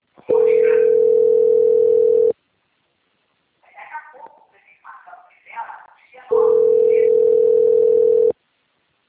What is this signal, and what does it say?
Ringback tone (North America), 440 + 480 Hz.